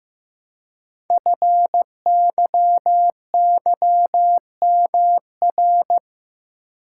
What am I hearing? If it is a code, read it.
Morse code, heard as a keyed tone, about 15 wpm, 700 Hz. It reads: FYYMR
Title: Code Groups - 5 Letter:Numbers - 15wpm